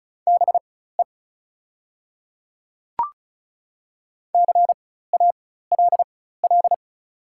Morse code, heard as a keyed tone, about 34 wpm, spaced as for 16 wpm, 700 Hz. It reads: BE CALL
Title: 4x Character Spacing - Top 100 Words - 35wpm